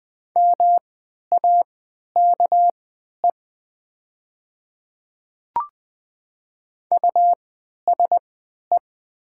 Morse code, heard as a keyed tone, about 20 wpm, spaced as for 11 wpm, 690 Hz. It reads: MAKE USE
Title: 3x Character Spacing - Top 200 Words - 20wpm